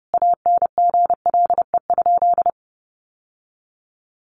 Morse code, heard as a keyed tone, about 30 wpm, 700 Hz. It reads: ANGLE?
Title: Sentences from Top 1500 Words 30wpm